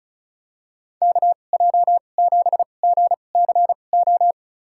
Morse code, heard as a keyed tone, about 36 wpm, spaced as for 25 wpm, 700 Hz. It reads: KJ7GCO
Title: USA Call Signs - 2x Character Spacing - 35wpm